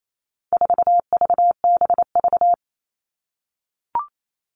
Morse code, heard as a keyed tone, about 28 wpm, 700 Hz. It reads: 4VBV
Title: All Cut Numbers - 4-Digit 28wpm